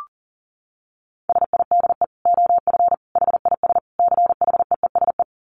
Morse code, heard as a keyed tone, about 40 wpm, 700 Hz. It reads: SIDE OF HIS CHEESE
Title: Dit Challenge 40wpm